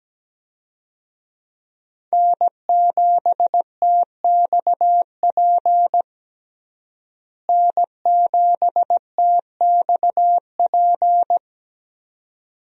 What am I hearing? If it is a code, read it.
Morse code, heard as a keyed tone, about 17 wpm, 700 Hz. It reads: N7TXP N7TXP
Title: USA Call Signs - 5 Characters - Repeated 2x - 17wpm